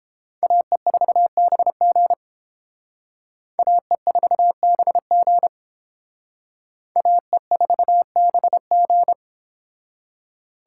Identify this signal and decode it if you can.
Morse code, heard as a keyed tone, about 30 wpm, 700 Hz. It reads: AE4BG AE4BG AE4BG